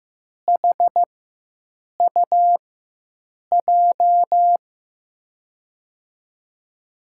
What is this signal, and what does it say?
Morse code, heard as a keyed tone, about 15 wpm, spaced as for 7 wpm, 700 Hz. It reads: HUJ
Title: USA Call Signs - 4x Character Spacing - 15wpm